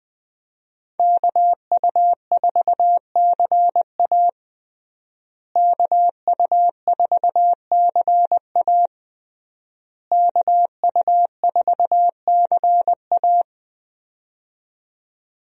Morse code, heard as a keyed tone, about 20 wpm, 700 Hz. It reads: KU4CA KU4CA KU4CA
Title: USA Call Signs - 5 Characters - Repeated 3x - 20wpm